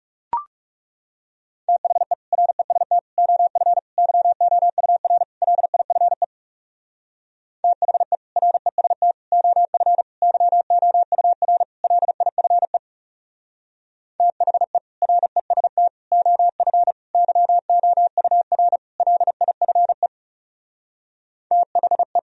Morse code, heard as a keyed tone, about 40 wpm, 700 Hz. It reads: THE REST OF YOUR LIFE THE REST OF YOUR LIFE THE REST OF YOUR LIFE THE